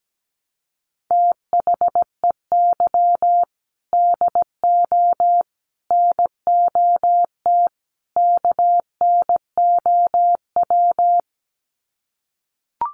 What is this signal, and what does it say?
Morse code, heard as a keyed tone, about 17 wpm, 700 Hz. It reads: THEY DO NOT KNOW
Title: Sets of 4 Words 17wpm